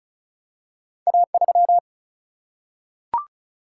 Morse code, heard as a keyed tone, about 36 wpm, 700 Hz. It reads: A3
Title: Single-Letter Single-Number 35wpm